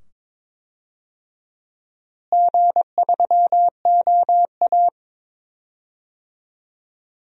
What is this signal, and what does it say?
Morse code, heard as a keyed tone, about 22 wpm, 700 Hz. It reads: G3OA
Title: DXCC-100 Call Signs - 4 Characters - Repeated 2x - 22wpm